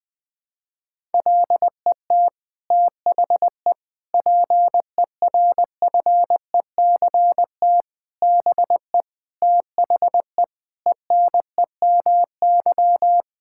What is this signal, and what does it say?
Morse code, heard as a keyed tone, about 20 wpm, 700 Hz. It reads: LET THE PERFECT BE THE ENEMY
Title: Sets of 9 Words - 20wpm